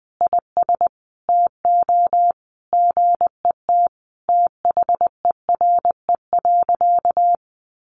Morse code, heard as a keyed tone, about 20 wpm, 700 Hz. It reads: IS TO GET THERE.
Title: Sentences from Top 1500 Words 20wpm